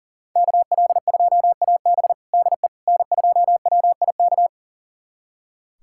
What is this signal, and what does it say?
Morse code, heard as a keyed tone, about 40 wpm, 700 Hz. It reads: KL2AB DE N2WIK
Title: USA Callsign DE Callsign - 5 Characters - 40wpm